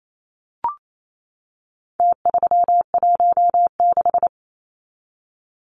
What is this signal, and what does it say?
Morse code, heard as a keyed tone, about 28 wpm, 700 Hz. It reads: T316
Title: Numbers - Contest Style - 4-Digit 28wpm